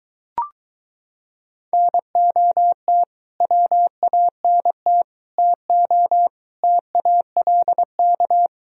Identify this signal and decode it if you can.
Morse code, heard as a keyed tone, about 23 wpm, 700 Hz. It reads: NOT WANT TO TALK